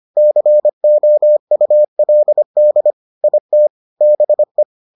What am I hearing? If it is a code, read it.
Morse code, heard as a keyed tone, about 25 wpm, 600 Hz. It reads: COULD IT BE